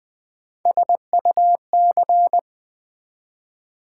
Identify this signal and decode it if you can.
Morse code, heard as a keyed tone, about 20 wpm, 700 Hz. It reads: SUC
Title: Build A Word - 3 Letters At A Time - 9-Letter Words - 20wpm